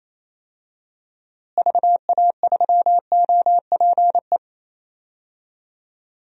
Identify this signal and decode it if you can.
Morse code, heard as a keyed tone, about 28 wpm, 700 Hz. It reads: VA3OPE
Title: DXCC-100 Call Signs - 6 Characters - Repeated 2x - 28wpm